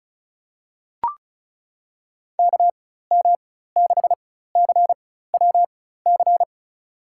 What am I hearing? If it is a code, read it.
Morse code, heard as a keyed tone, about 36 wpm, spaced as for 16 wpm, 700 Hz. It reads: KM6CWC
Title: USA Call Signs - 6 Characters - 4x Character Spacing - 35wpm